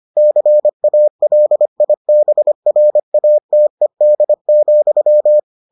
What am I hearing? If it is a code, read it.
Morse code, heard as a keyed tone, about 25 wpm, 600 Hz. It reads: CALIBRATED,